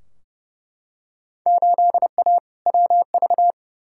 Morse code, heard as a keyed tone, about 31 wpm, 700 Hz. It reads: 8A WV
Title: Field Day Exchanges Encore - 30wpm